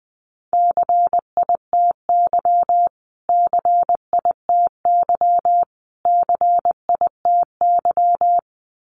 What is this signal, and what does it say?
Morse code, heard as a keyed tone, about 20 wpm, 700 Hz. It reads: CITY CITY CITY
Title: Top 500 Words Repeated 20wpm